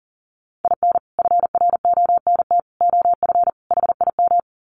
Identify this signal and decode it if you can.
Morse code, heard as a keyed tone, about 40 wpm, 700 Hz. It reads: IN FRONT OF HIM
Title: Sets of 4 Words 40wpm